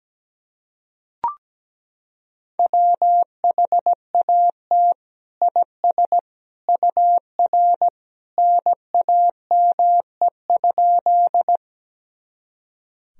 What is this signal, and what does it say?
Morse code, heard as a keyed tone, about 17 wpm, 700 Hz. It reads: WHAT IS UR NAME?
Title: Sentences from Top 200 Words and Abbreviations 17wpm